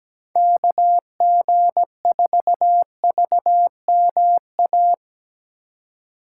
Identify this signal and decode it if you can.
Morse code, heard as a keyed tone, about 17 wpm, 700 Hz. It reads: KG4VMA
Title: USA Call Signs - 6 Characters - 2x Character Spacing - 17wpm